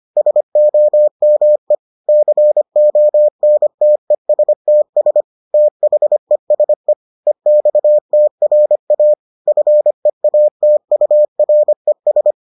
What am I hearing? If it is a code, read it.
Morse code, heard as a keyed tone, about 25 wpm, 600 Hz. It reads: SOME CONTESTS THESE EXTRA FEATURES